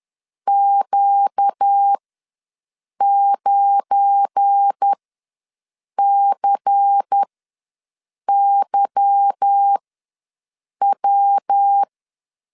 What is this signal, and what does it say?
Morse code, heard as a keyed tone, about 10 wpm, spaced as for 6 wpm, 800 Hz. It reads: Q9CYW
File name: LEZIONI / 9 2 ws 10wpm 4130